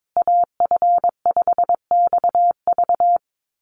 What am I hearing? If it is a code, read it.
Morse code, heard as a keyed tone, about 22 wpm, 700 Hz. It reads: AF5XV
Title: USA Call Signs Repeated 2x 22wpm